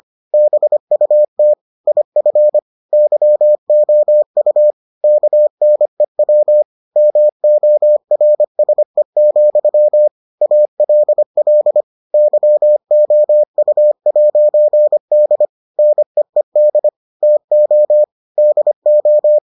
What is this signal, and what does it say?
Morse code, heard as a keyed tone, about 25 wpm, 600 Hz. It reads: BUT IF YOU KNEW MORSE, ALL YOU'D NEED TO DO